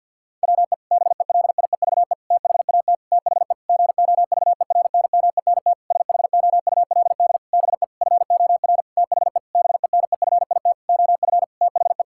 Morse code, heard as a keyed tone, about 50 wpm, 700 Hz. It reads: WE BELIEVE THAT THE GOVERNMENT SHOULD BE FOR THE BENEFIT OF THE
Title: Sentences from Top 1000 Words - Review Part 6 - 50wpm